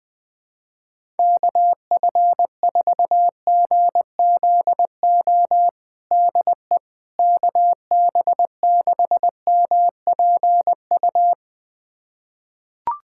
Morse code, heard as a keyed tone, about 20 wpm, 700 Hz. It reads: KF4GZO DE KB6MPU
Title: USA Callsign DE Callsign - 6 Characters - 20wpm